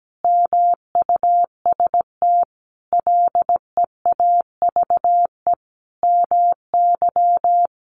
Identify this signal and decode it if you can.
Morse code, heard as a keyed tone, about 17 wpm, 700 Hz. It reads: MUST LEAVE MY
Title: Sentences from Top 1500 Words 17wpm